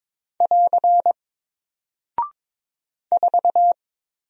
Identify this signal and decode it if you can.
Morse code, heard as a keyed tone, about 22 wpm, 700 Hz. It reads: <AR> 4
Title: Single Character (Letter:Number:Punctuation:Prosign) Alt 22wpm